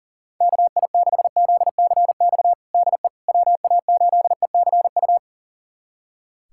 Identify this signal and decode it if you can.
Morse code, heard as a keyed tone, about 37 wpm, 700 Hz. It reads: KI6ZCX DE WA8ECU
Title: USA Callsign DE Callsign - 6 Characters - 40wpm